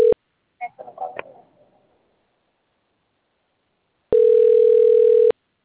Ringback tone (North America), 440 + 480 Hz.
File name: out- - - - - .80304